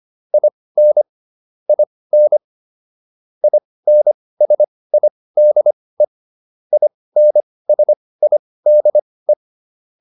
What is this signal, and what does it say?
Morse code, heard as a keyed tone, about 25 wpm, spaced as for 18 wpm, 600 Hz. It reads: IN IN INSIDE INSIDE